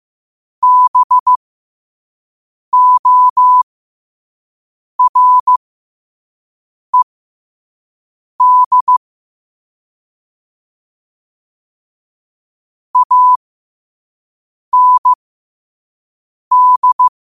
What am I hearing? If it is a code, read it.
Morse code, heard as a keyed tone, about 15 wpm, spaced as for 5 wpm, 1000 Hz. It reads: BORED AND
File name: qotdfiles / 5wpm